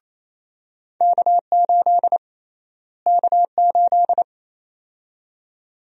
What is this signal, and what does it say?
Morse code, heard as a keyed tone, about 27 wpm, 700 Hz. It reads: K8 K8
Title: USA Call Sign Prefixes - Repeated 2x - 28wpm